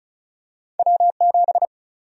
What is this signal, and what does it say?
Morse code, heard as a keyed tone, about 36 wpm, 700 Hz. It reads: W7
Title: USA Call Sign Prefixes - 35wpm